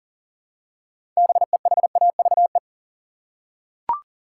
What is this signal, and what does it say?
Morse code, heard as a keyed tone, about 38 wpm, 700 Hz. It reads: BEHAVE
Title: Six Letter Words - 40wpm